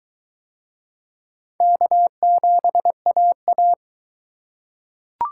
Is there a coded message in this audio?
Yes: Morse code from 1.6 to 3.7 s, heard as a keyed tone, about 19 wpm, 700 Hz. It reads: K7AA